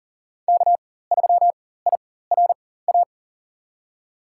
Morse code, heard as a keyed tone, about 40 wpm, spaced as for 19 wpm, 700 Hz. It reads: K3IRA
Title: USA Call Signs - 5 Characters - 4x Character Spacing - 40wpm